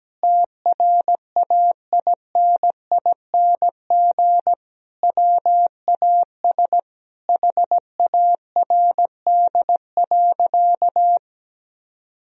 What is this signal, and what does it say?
Morse code, heard as a keyed tone, about 17 wpm, 700 Hz. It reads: TRAINING WAS HARD.
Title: Sentences from Top 1000 Words - Review Part 4 - 17wpm